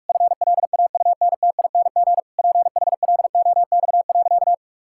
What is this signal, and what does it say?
Morse code, heard as a keyed tone, about 46 wpm, 700 Hz. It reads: FLAUNTING PHLOX.
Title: Pangram Challenge v2 - 45wpm